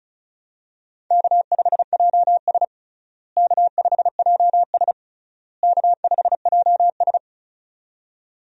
Morse code, heard as a keyed tone, about 36 wpm, 700 Hz. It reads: K5JS K5JS K5JS
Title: USA Call Signs - 4 Characters - Repeated 3x - 35wpm